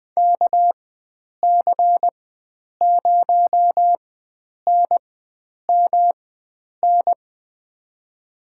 Morse code, heard as a keyed tone, about 20 wpm, spaced as for 9 wpm, 700 Hz. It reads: KC0NMN
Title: USA Call Signs - 6 Characters - 4x Character Spacing - 20wpm